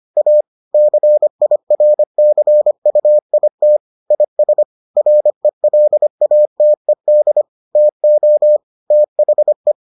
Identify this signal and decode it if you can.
Morse code, heard as a keyed tone, about 25 wpm, 600 Hz. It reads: A CIRCUIT IS RELATED TO THE